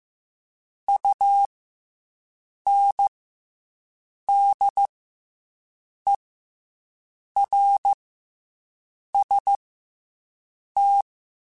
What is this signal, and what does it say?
Morse code, heard as a keyed tone, about 15 wpm, 800 Hz. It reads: U N D E R S T
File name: Special INGs / Special INGs-mp3s / SlowToFaster ing @13